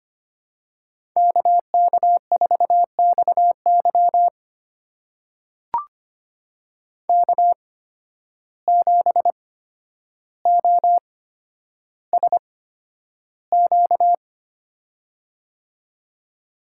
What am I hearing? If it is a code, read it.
Morse code, heard as a keyed tone, about 25 wpm, spaced as for 11 wpm, 700 Hz. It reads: KK4XY K7OSQ
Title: USA Call Signs - 5 Characters - 8x Character Spacing - 25wpm